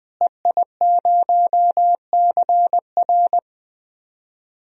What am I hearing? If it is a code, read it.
Morse code, heard as a keyed tone, about 21 wpm, 700 Hz. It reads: EI0CR
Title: DXCC-100 Call Signs - 5 Characters - Repeated 2x - 20wpm